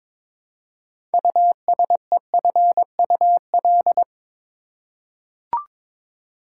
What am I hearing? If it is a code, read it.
Morse code, heard as a keyed tone, about 22 wpm, 700 Hz. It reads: USEFUL
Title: Six Letter Words - 22wpm